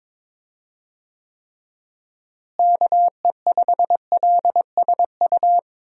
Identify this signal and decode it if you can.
Morse code, heard as a keyed tone, about 22 wpm, 700 Hz. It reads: KE5LSU
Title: USA Call Signs - 6 Characters - Repeated 2x - 22wpm